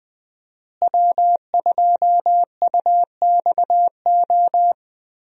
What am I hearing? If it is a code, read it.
Morse code, heard as a keyed tone, about 20 wpm, 700 Hz. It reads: W2UXO
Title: USA Call Signs - 5 Characters - Repeated 2x - 20wpm